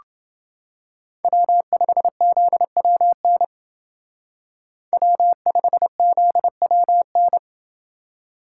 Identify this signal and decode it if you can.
Morse code, heard as a keyed tone, about 29 wpm, 700 Hz. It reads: W5ZWN W5ZWN